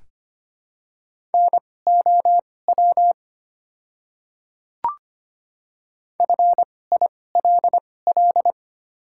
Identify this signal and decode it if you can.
Morse code, heard as a keyed tone, about 25 wpm, spaced as for 18 wpm, 700 Hz. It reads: NOW FILL
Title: 2x Character Spacing - Top 500 Words - 25wpm